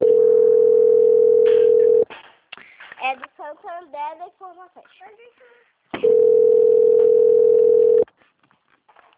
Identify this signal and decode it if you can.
Ringback tone (North America), 440 + 480 Hz, with speech.
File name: out- - - - - .24966